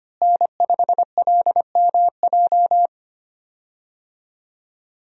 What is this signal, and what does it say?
Morse code, heard as a keyed tone, about 25 wpm, 700 Hz. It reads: N5LMJ
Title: USA Call Signs - 5 Characters - Repeated 3x - 25wpm